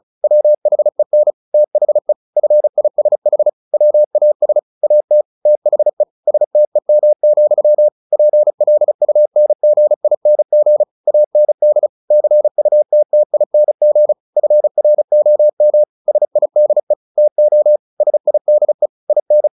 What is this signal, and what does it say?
Morse code, heard as a keyed tone, about 34 wpm, 600 Hz. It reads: WHEN THE FISH WAS AT THE STEM, PLUNGING AND CUTTING FROM SIDE TO SIDE IN
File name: the old man and the sea 2 35wpm